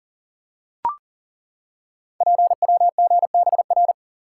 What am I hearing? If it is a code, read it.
Morse code, heard as a keyed tone, about 39 wpm, 700 Hz. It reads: PWGBR
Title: Code Groups - 5 Letters - 40wpm